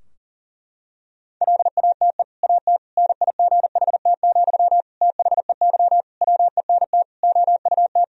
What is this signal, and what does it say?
Morse code, heard as a keyed tone, about 40 wpm, 700 Hz. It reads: LATE AT NIGHT, THEY WENT OUT
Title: Sentences from Top 1000 Words - Review Part 3 - 40wpm